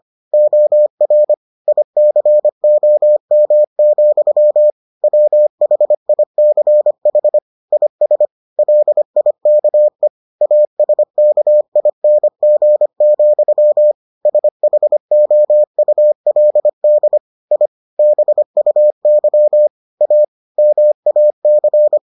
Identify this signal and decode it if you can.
Morse code, heard as a keyed tone, about 25 wpm, 600 Hz. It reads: OR ICOM, WHICH IS LIKE ASKING, SHOULD I BUY A MAC